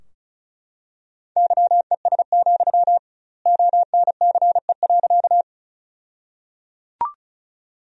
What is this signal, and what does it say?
Morse code, heard as a keyed tone, about 36 wpm, 700 Hz. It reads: YES, ONCE.
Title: 2x Word-Spacing - Sentences from Top 500 Words - 35wpm